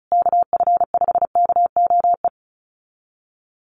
Morse code, heard as a keyed tone, about 34 wpm, 700 Hz. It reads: KF5KOE
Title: USA Call Signs Repeated 3x 35wpm